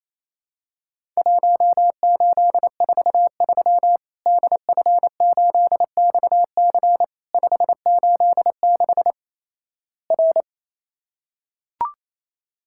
Morse code, heard as a keyed tone, about 28 wpm, 700 Hz. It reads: 1843 DF8XC 586 R
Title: Worked All Europe DX Contest - QTC Exchange - 28wpm